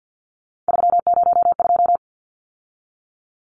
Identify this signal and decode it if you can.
Morse code, heard as a keyed tone, about 50 wpm, 700 Hz. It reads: 302
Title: Numbers - 3-Digit 50wpm